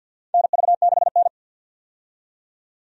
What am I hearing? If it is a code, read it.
Morse code, heard as a keyed tone, about 52 wpm, 700 Hz. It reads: NV6N
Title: USA Call Signs - 4 Characters - 50wpm